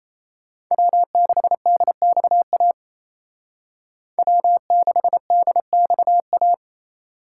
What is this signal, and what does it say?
Morse code, heard as a keyed tone, about 36 wpm, 700 Hz. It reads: W6DXA W6DXA